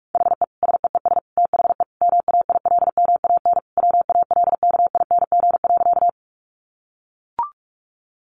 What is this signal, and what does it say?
Morse code, heard as a keyed tone, about 44 wpm, 700 Hz. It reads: HE SEES THE MAILMAN WALKING.
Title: Additional Easy Phrases 45wpm